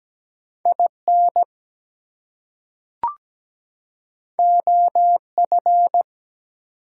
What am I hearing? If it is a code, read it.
Morse code, heard as a keyed tone, about 17 wpm, 700 Hz. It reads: IN OF
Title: Two Letter Words - 17wpm